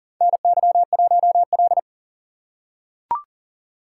Morse code, heard as a keyed tone, about 38 wpm, 700 Hz. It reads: NY1L
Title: USA Call Signs - 4 Characters - 3x Character Spacing - 40wpm